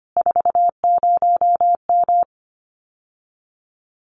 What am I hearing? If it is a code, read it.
Morse code, heard as a keyed tone, about 25 wpm, 700 Hz. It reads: 40M
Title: Common QSO Phrases Encore 25wpm